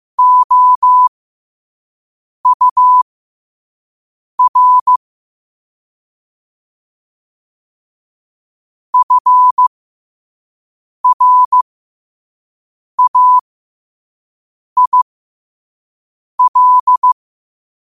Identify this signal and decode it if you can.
Morse code, heard as a keyed tone, about 15 wpm, spaced as for 5 wpm, 1000 Hz. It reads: OUR FRAIL
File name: qotdfiles / 5wpm